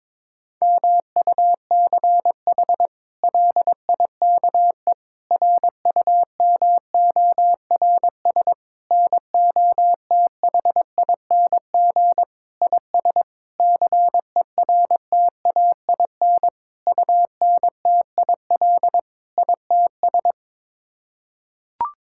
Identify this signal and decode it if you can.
Morse code, heard as a keyed tone, about 22 wpm, 700 Hz. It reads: MUCH LIKE RUMORS NOTHING IS CERTAIN UNTIL ITS
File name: practice / 22wpm / Sets of 8 Words - 22wpm